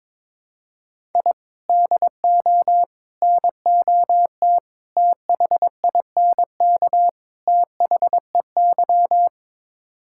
Morse code, heard as a keyed tone, about 22 wpm, 700 Hz. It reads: I DO NOT THINK THEY
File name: practice / 22wpm / Build A Phrase - 5-Word Phrases - 22wpm